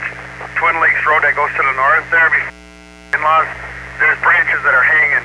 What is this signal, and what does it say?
Emergency Alert System (SAME) tones, heard as harsh buzzing bursts, with speech. It is heard from 0.5 to 4.3 s.